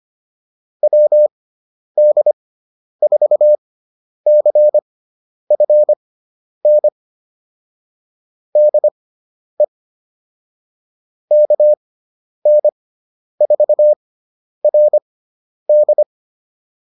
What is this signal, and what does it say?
Morse code, heard as a keyed tone, about 25 wpm, spaced as for 10 wpm, 600 Hz. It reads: WD4CFN DE KN4RD
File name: QSO207 10